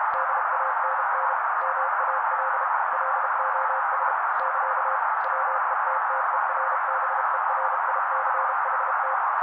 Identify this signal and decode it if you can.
Morse code, heard as a keyed tone, about 32 wpm, 530 Hz. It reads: DANN MAL LOS <KA> WETTERBERICHT
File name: daily / 31wpm noise